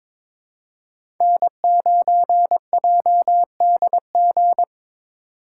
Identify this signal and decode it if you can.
Morse code, heard as a keyed tone, about 23 wpm, 700 Hz. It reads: N9JDG
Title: USA Call Signs - 6x Character Spacing - 22wpm